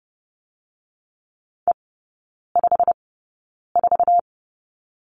Morse code, heard as a keyed tone, about 30 wpm, spaced as for 9 wpm, 700 Hz. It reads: E54
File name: course / 30:10wpm / 087 Lesson 15e - Number 5 - Three Characters 30wpm